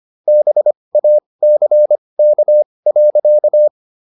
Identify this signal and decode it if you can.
Morse code, heard as a keyed tone, about 25 wpm, spaced as for 20 wpm, 600 Hz. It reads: BACK.